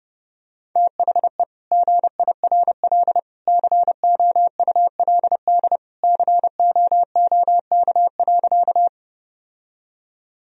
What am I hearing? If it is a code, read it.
Morse code, heard as a keyed tone, about 31 wpm, 700 Hz. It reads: THE GIRL COULD COOK.
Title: Sentences from Top 1000 Words - Review Part 4 - 30wpm